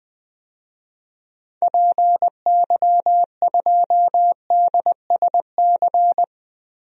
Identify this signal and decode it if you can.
Morse code, heard as a keyed tone, about 21 wpm, 700 Hz. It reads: PY2DSC
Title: DXCC-100 Call Signs - 6 Characters - Repeated 2x - 20wpm